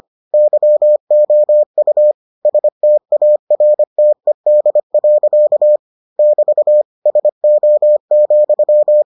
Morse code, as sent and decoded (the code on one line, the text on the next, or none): -.-- --- ..- / ... - .- .-. - . -.. .-.-.- / -...- / ... --- --..--
YOU STARTED. <BT> SO,